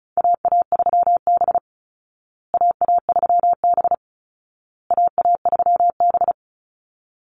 Morse code, heard as a keyed tone, about 34 wpm, 700 Hz. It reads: AA3B AA3B AA3B
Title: Call Signs Repeated 2x - 35wpm